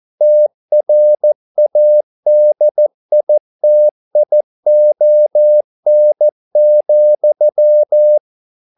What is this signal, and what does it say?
Morse code, heard as a keyed tone, about 14 wpm, 600 Hz. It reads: TRADITION,